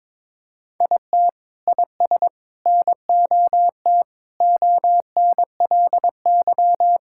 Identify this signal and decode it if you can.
Morse code, heard as a keyed tone, about 22 wpm, 700 Hz. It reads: IT IS NOT ONLY
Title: Sentences from Top 1600 Words 22wpm